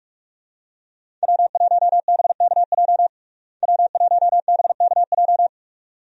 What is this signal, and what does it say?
Morse code, heard as a keyed tone, about 46 wpm, 700 Hz. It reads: W1BKJ W1BKJ
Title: USA Call Signs - 5 Characters - Repeated 2x - 45wpm